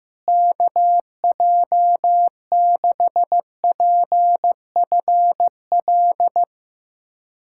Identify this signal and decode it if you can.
Morse code, heard as a keyed tone, about 15 wpm, 700 Hz. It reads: KJ6PFL
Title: USA Callsign DE Callsign - 6 Characters - 15wpm